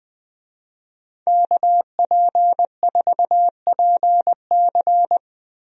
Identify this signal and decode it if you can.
Morse code, heard as a keyed tone, about 20 wpm, 700 Hz. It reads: KP4PC
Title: DXCC-100 Call Signs - 5 Characters - Repeated 3x - 20wpm